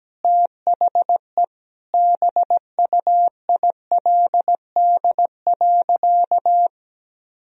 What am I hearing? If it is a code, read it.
Morse code, heard as a keyed tone, about 17 wpm, 700 Hz. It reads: THE BUILD.
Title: Sentences from Top 1000 Words - Review Part 1 - 17wpm